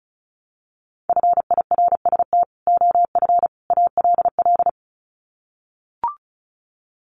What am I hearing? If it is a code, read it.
Morse code, heard as a keyed tone, about 34 wpm, 700 Hz. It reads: FIRST OF ALL
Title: Sets of 3 Words 35wpm